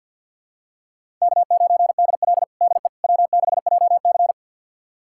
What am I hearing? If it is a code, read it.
Morse code, heard as a keyed tone, about 53 wpm, 700 Hz. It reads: K9DL DE W6JC
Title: USA Callsign DE Callsign - 4 Characters - 50wpm